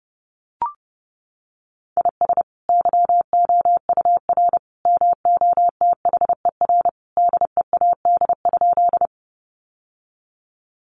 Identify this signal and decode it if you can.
Morse code, heard as a keyed tone, about 30 wpm, 700 Hz. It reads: IS YOUR MOTHER DEAD?
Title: Sentences from Top 1500 Words 30wpm